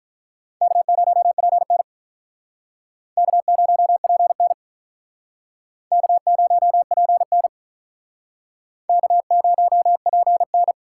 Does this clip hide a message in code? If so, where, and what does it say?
Yes: Morse code from 0.6 to 10.7 s, heard as a keyed tone, about 45 wpm, 700 Hz. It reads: K0PN K0PN K0PN K0PN